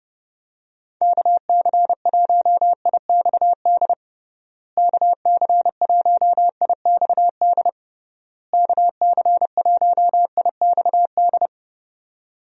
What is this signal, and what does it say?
Morse code, heard as a keyed tone, about 32 wpm, 700 Hz. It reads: KC1IXD KC1IXD KC1IXD
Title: USA Call Signs - 6 Characters - Repeated 3x - 30wpm